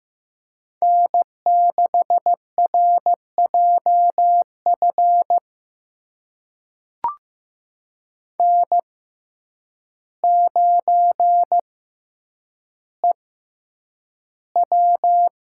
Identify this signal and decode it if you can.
Morse code, heard as a keyed tone, about 15 wpm, spaced as for 8 wpm, 700 Hz. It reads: N6RJF N9EW
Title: USA Call Signs - 6x Character Spacing - 15wpm